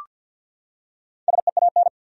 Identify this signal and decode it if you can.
Morse code, heard as a keyed tone, about 52 wpm, 700 Hz. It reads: SEAN
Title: USA Common Names - Max 5 Letters - 50wpm